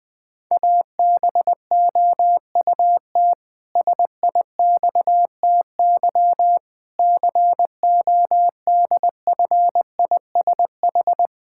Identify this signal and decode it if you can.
Morse code, heard as a keyed tone, about 20 wpm, 700 Hz. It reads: ABOUT SIXTY CODFISH
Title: Pangram Challenge v2 - 20wpm